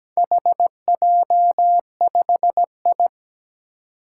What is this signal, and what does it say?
Morse code, heard as a keyed tone, about 17 wpm, 700 Hz. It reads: HJ5I
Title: Four Challenging Letter-Number Combinations 17wpm